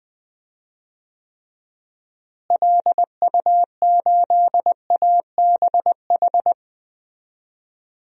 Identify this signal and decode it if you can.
Morse code, heard as a keyed tone, about 20 wpm, 700 Hz. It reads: LU8ABH
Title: DXCC-100 Call Signs - 6 Characters - Repeated 2x - 20wpm